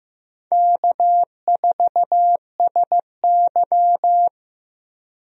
Morse code, heard as a keyed tone, about 15 wpm, 700 Hz. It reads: K4SY
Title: USA Call Signs - 4 Characters - Repeated 2x - 15wpm